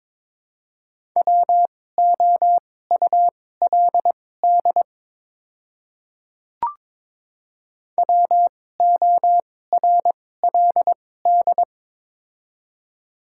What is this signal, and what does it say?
Morse code, heard as a keyed tone, about 22 wpm, spaced as for 16 wpm, 700 Hz. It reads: WOULD WORLD
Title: 2x Character Spacing - Top 200 Words - 22wpm